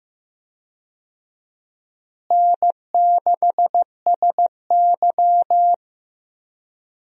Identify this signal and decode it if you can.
Morse code, heard as a keyed tone, about 15 wpm, 700 Hz. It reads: N6SY
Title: USA Call Signs - 4 Characters - Repeated 3x - 15wpm